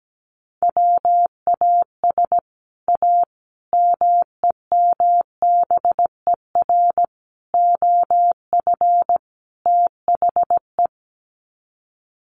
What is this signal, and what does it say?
Morse code, heard as a keyed tone, about 17 wpm, 700 Hz. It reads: WAS A MEMBER OF THE
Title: Sets of 5 Words 17wpm